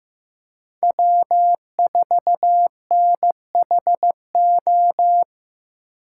Morse code, heard as a keyed tone, about 15 wpm, 700 Hz. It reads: W4NHO
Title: USA Call Signs - 8x Character Spacing - 15wpm